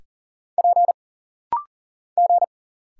Morse code, heard as a keyed tone, about 40 wpm, 700 Hz. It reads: P G
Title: Single Letters - Rapid-Fire 40wpm